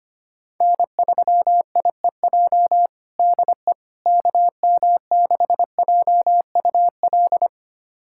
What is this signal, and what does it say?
Morse code, heard as a keyed tone, about 25 wpm, 700 Hz. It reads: N3IEJ DE KM6JUL
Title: USA Callsign DE Callsign 25wpm